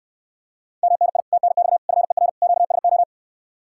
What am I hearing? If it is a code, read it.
Morse code, heard as a keyed tone, about 69 wpm, 700 Hz. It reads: MNI TNX FER QSO